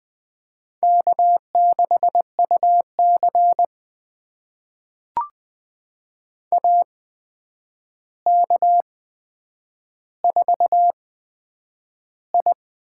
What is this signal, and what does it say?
Morse code, heard as a keyed tone, about 20 wpm, spaced as for 9 wpm, 700 Hz. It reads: K6UC AK4I
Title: USA Call Signs - 4 Characters - 8x Character Spacing - 20wpm